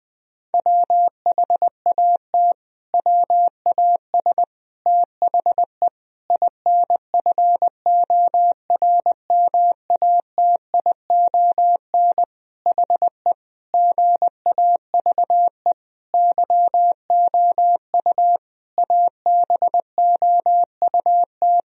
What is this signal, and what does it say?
Morse code, heard as a keyed tone, about 21 wpm, 700 Hz. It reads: WHAT WAS THE INFORMATION HE GAVE YOU ABOUT
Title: Sentences from Top 1000 Words - Review Part 3 - 20wpm